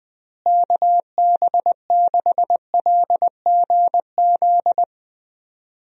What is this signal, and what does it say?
Morse code, heard as a keyed tone, about 20 wpm, 700 Hz. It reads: KB6LGZ
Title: USA Call Signs - 6 Characters - Repeated 2x - 20wpm